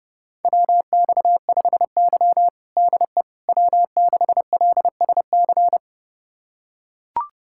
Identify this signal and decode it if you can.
Morse code, heard as a keyed tone, about 30 wpm, 700 Hz. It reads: WX5Y DE W6LSC
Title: USA Callsign DE Callsign 30wpm